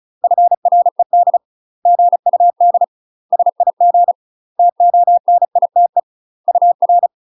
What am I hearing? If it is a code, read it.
Morse code, heard as a keyed tone, about 35 wpm, 700 Hz. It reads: FRED GUD SIG TONITE UR
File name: UK MASTER QSOs TRAINING / 35WPM / UK MASTER QSOs 0016